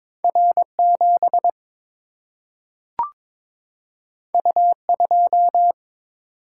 Morse code, heard as a keyed tone, about 22 wpm, 700 Hz. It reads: R7 U2
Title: Single-Letter Single-Number 22wpm